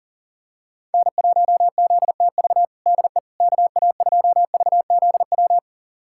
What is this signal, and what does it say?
Morse code, heard as a keyed tone, about 40 wpm, 700 Hz. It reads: N1ZTV DE KA2VZW
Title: USA Callsign DE Callsign 40wpm